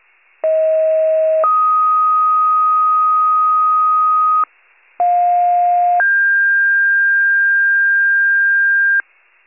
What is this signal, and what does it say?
Two-tone page, heard as a one-second tone, then a three-second tone, about 640 then 1250 Hz, about 710 then 1600 Hz, with noise. It is heard from 0.4 to 9.0 s.